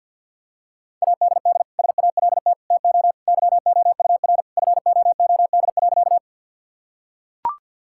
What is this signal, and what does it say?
Morse code, heard as a keyed tone, about 45 wpm, 700 Hz. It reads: ADD SALT TO YOUR FOOD.
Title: Sentences from Top 1000 Words - Review Part 5 - 50wpm